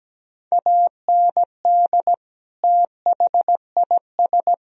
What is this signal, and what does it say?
Morse code, heard as a keyed tone, about 17 wpm, 700 Hz. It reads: AND THIS
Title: Sentences from Top 1000 Words - Review Part 1 - 17wpm